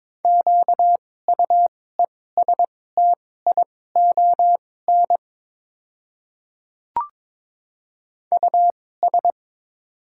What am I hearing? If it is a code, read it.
Morse code, heard as a keyed tone, about 23 wpm, spaced as for 16 wpm, 700 Hz. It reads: QUESTION US
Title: 2x Character Spacing - Top 500 Words - 22wpm